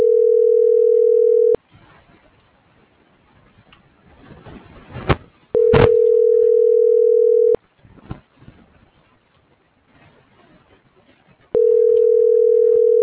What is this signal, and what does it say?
Ringback tone (North America), 440 + 480 Hz.